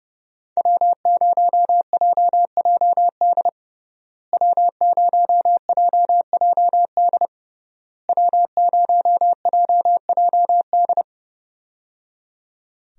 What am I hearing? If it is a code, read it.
Morse code, heard as a keyed tone, about 30 wpm, 700 Hz. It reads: W0JJD W0JJD W0JJD
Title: USA Call Signs - 5 Characters - Repeated 3x - 30wpm